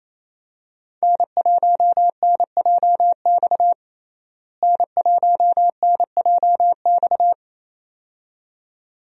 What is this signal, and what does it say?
Morse code, heard as a keyed tone, about 28 wpm, 700 Hz. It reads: N1NJX N1NJX
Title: USA Call Signs - 5 Characters - Repeated 2x - 28wpm